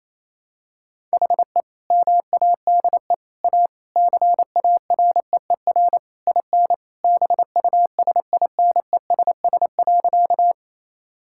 Morse code, heard as a keyed tone, about 28 wpm, 700 Hz. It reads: HE MADE A CAREER IN BUSINESS.
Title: Sentences from Top 1000 Words - Review Part 6 - 28wpm